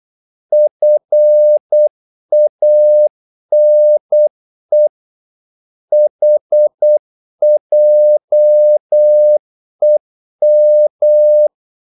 Morse code, heard as a keyed tone, about 8 wpm, 600 Hz. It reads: FANE HJEM
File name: Korte ord DK 8wpm